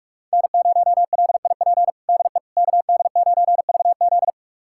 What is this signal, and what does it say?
Morse code, heard as a keyed tone, about 46 wpm, 700 Hz. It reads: N0LIP DE KD9VZ
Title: USA Callsign DE Callsign - 5 Characters - 45wpm